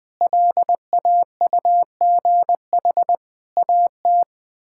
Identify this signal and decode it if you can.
Morse code, heard as a keyed tone, about 20 wpm, 700 Hz. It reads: LAUGH AT
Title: Sets of 8 Words - 20wpm